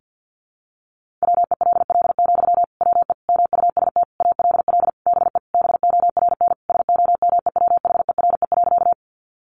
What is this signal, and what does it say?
Morse code, heard as a keyed tone, about 50 wpm, 700 Hz. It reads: WELL, WE MUST ALL BE BORN SOMEWHERE.
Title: Sentences from Top 1400 Words 50wpm